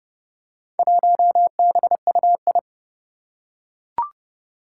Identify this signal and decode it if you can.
Morse code, heard as a keyed tone, about 30 wpm, 700 Hz. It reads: 1BUI
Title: Four Challenging Letter-Number Combinations 30wpm